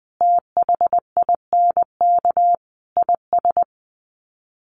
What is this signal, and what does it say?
Morse code, heard as a keyed tone, about 20 wpm, 700 Hz. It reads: THINK IS
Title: Sets of 5 Words Encore 20wpm